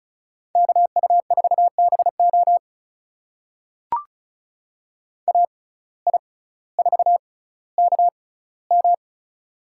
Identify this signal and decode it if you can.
Morse code, heard as a keyed tone, about 36 wpm, spaced as for 18 wpm, 700 Hz. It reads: KU4BO AI4KM
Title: USA Call Signs - 5 Characters - 6x Character Spacing - 35wpm